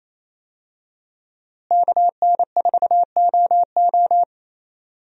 Morse code, heard as a keyed tone, about 28 wpm, 700 Hz. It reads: KN4OO
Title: USA Call Signs - 5 Characters - Repeated 3x - 28wpm